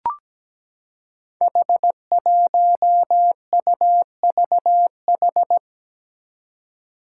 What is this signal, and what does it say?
Morse code, heard as a keyed tone, about 17 wpm, 700 Hz. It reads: H1UVH
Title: Five Challenging Letter-Number Combinations 17wpm